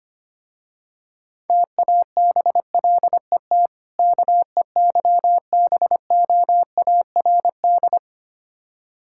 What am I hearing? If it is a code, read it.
Morse code, heard as a keyed tone, about 25 wpm, 700 Hz. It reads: TABLET KEYBOARD
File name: practice / 25wpm / Difficult Words with Hints - 25wpm